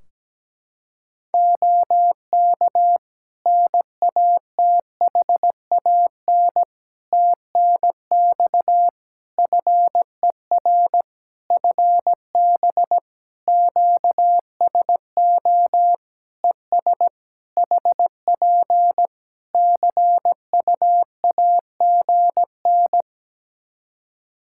Morse code, heard as a keyed tone, about 17 wpm, 700 Hz. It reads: OK NATHAN TNX FER FB QSO ES HP CUAGN
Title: Practice QSO - Exchange 4 - Ending - 17wpm